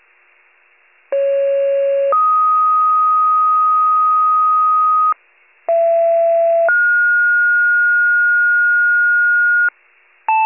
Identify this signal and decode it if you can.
Two-tone page, heard as a one-second tone, then a three-second tone, about 550 then 1250 Hz, about 670 then 1430 Hz, with noise. It is heard from 1.1 to 9.7 s.
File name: pager / 28 FIRE / wcalert